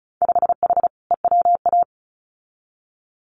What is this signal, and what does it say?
Morse code, heard as a keyed tone, about 36 wpm, 700 Hz. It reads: 5H EWA
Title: Winter Field Day Exchanges Encore 35wpm